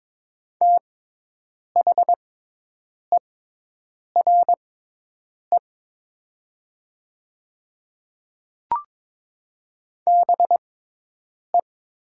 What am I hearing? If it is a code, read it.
Morse code, heard as a keyed tone, about 22 wpm, spaced as for 8 wpm, 700 Hz. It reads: THERE BE
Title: 6x Character Spacing - Top 100 Words - 22wpm